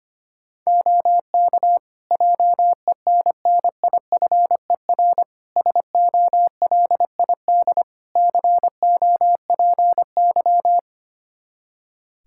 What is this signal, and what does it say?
Morse code, heard as a keyed tone, about 25 wpm, 700 Hz. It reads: OK JENNIFER SOLID COPY
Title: Practice QSO - Exchange 3 - More Info - 25wpm